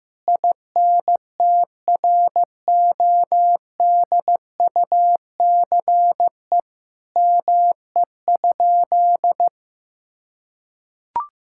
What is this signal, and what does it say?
Morse code, heard as a keyed tone, about 15 wpm, 700 Hz. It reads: INTRODUCE ME?
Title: Sentences from Top 1600 Words 15wpm